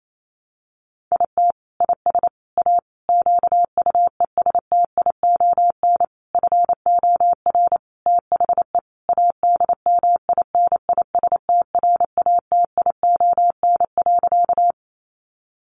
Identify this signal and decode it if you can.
Morse code, heard as a keyed tone, about 28 wpm, 700 Hz. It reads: IT IS A QUESTION FOR THE ADMINISTRATION.
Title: Sentences from Top 1300 Words 28wpm